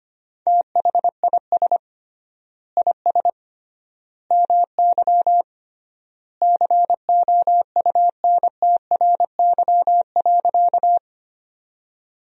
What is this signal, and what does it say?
Morse code, heard as a keyed tone, about 25 wpm, 700 Hz. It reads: THIS IS MY COUNTRY.
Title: 3x Word-Spacing - Sentences from Top 300 Words - 25wpm